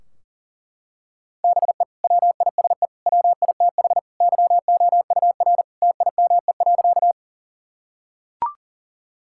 Morse code, heard as a keyed tone, about 40 wpm, 700 Hz. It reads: BE WISE WITH YOUR TIME.
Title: Sentences from Top 1600 Words 40wpm